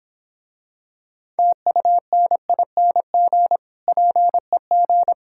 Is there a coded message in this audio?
Yes: Morse code from 1.4 to 5.1 s, heard as a keyed tone, about 26 wpm, 700 Hz. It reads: TUNING PEG